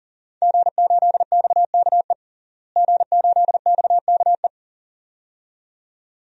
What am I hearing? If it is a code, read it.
Morse code, heard as a keyed tone, about 39 wpm, 700 Hz. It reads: G8XKE G8XKE
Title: DXCC-100 Call Signs - 5 Characters - Repeated 2x - 40wpm